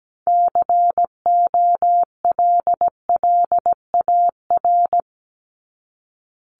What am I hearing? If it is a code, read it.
Morse code, heard as a keyed tone, about 17 wpm, 700 Hz. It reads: COLLAR
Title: Top 1751-2000 Words 17wpm